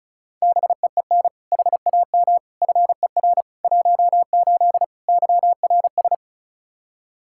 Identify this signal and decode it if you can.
Morse code, heard as a keyed tone, about 36 wpm, 700 Hz. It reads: BEEN HAM FER 18 YRS
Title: Practice QSO - Exchange 3 - More Info - 35wpm